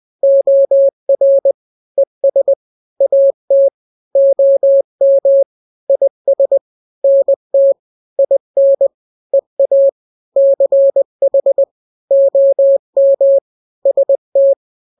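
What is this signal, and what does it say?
Morse code, heard as a keyed tone, about 20 wpm, 550 Hz. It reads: OR ES AT OM IS NT IN EA CH OM ST
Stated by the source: Morse drill of two-letter n-grams at 20/18 WPM (Farnsworth), from KK7RR.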